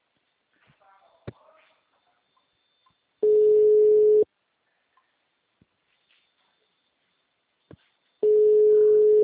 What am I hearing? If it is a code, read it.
Ringback tone (Europe, 425 Hz), with speech.